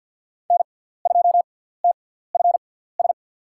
Morse code, heard as a keyed tone, about 51 wpm, spaced as for 17 wpm, 700 Hz. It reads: N2TFS
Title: USA Call Signs - 5 Characters - 6x Character Spacing - 50wpm